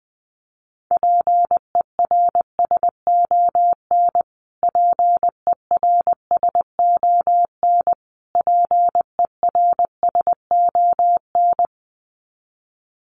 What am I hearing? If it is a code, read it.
Morse code, heard as a keyed tone, about 20 wpm, 700 Hz. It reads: PERSON PERSON PERSON
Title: Top 500 Words Repeated 20wpm